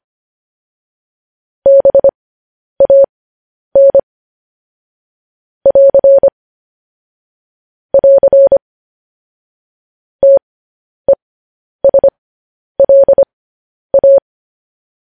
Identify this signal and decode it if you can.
Morse code, heard as a keyed tone, about 25 wpm, spaced as for 10 wpm, 550 Hz. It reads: BAN <AR> <AR> TESLA